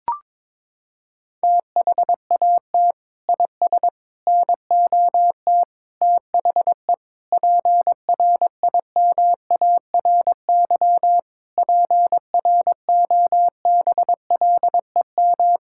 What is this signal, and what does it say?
Morse code, heard as a keyed tone, about 22 wpm, 700 Hz. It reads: THAT IS NOT THE PRIMARY PROBLEM
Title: Sentences from Top 1000 Words - Review Part 5 - 22wpm